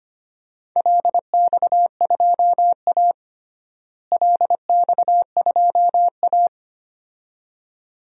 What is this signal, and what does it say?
Morse code, heard as a keyed tone, about 25 wpm, 700 Hz. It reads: LX2A LX2A
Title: DXCC-100 Call Signs - 4 Characters - Repeated 2x - 25wpm